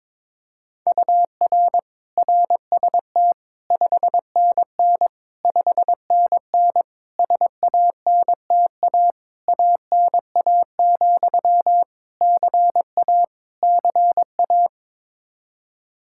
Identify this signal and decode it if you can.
Morse code, heard as a keyed tone, about 22 wpm, 700 Hz. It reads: UR RST 5NN 5NN SANTA ANA, CA CA
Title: Practice QSO - Exchange 1 - Beginning - 22wpm